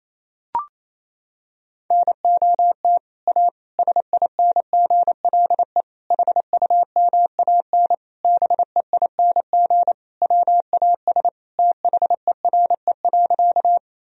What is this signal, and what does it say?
Morse code, heard as a keyed tone, about 28 wpm, 700 Hz. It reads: NOT A SINGLE HUMAN BEING WAS THERE.
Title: Sentences from Top 1000 Words - Review Part 4 - 28wpm